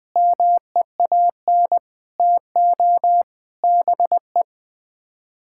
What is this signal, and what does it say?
Morse code, heard as a keyed tone, about 20 wpm, 700 Hz. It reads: MEAN TO BE
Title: Sets of 6 Words - 20wpm